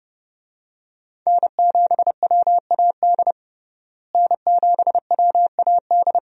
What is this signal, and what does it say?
Morse code, heard as a keyed tone, about 32 wpm, 700 Hz. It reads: N7WAD N7WAD
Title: USA Call Signs - 5 Characters - Repeated 2x - 30wpm